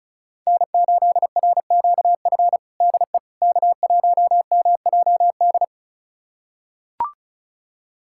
Morse code, heard as a keyed tone, about 36 wpm, 700 Hz. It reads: N8RQF DE K1MJD
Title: USA Callsign DE Callsign - 5 Characters - 35wpm